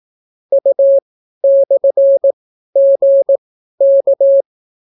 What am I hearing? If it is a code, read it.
Morse code, heard as a keyed tone, about 18 wpm, spaced as for 12 wpm, 550 Hz. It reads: U/GK